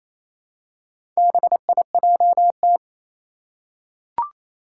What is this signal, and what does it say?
Morse code, heard as a keyed tone, about 27 wpm, 700 Hz. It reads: BIJT
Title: Four Challenging Letter-Number Combinations 28wpm